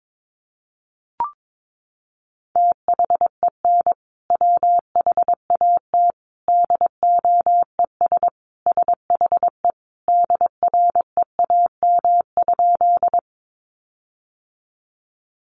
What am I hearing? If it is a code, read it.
Morse code, heard as a keyed tone, about 22 wpm, 700 Hz. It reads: THEN WHAT DOES SHE DREAM?
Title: Sentences from Top 1500 Words 22wpm